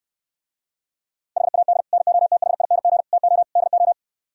Morse code, heard as a keyed tone, about 67 wpm, 700 Hz. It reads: HAD NOTHING TO DO